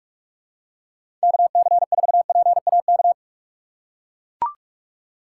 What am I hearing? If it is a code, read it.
Morse code, heard as a keyed tone, about 44 wpm, 700 Hz. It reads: KC4PAK
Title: USA Call Signs - 6 Characters - 3x Character Spacing - 45wpm